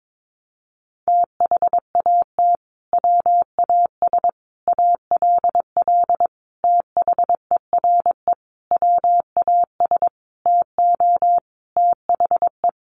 Morse code, heard as a keyed tone, about 22 wpm, 700 Hz. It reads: THAT WAS ALL THERE WAS TO THE
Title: Sentences from Top 1100 Words 22wpm